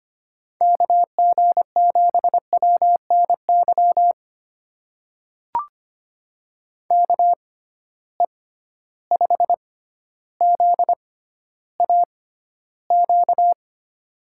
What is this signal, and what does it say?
Morse code, heard as a keyed tone, about 25 wpm, spaced as for 13 wpm, 700 Hz. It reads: KG7WNY KE5ZAQ
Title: USA Call Signs - 6 Characters - 6x Character Spacing - 25wpm